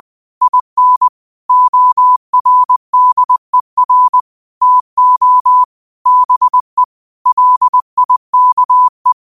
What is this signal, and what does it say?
Morse code, heard as a keyed tone, about 20 wpm, 1000 Hz. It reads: IN ORDER TO BE LIKE